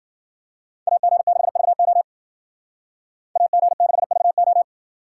Morse code, heard as a keyed tone, about 52 wpm, 700 Hz. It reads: AG6VO AG6VO